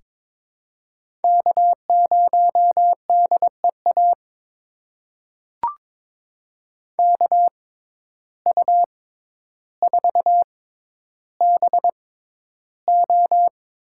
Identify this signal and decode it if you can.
Morse code, heard as a keyed tone, about 21 wpm, spaced as for 11 wpm, 700 Hz. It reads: K0DEA KU4BO
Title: USA Call Signs - 5 Characters - 6x Character Spacing - 22wpm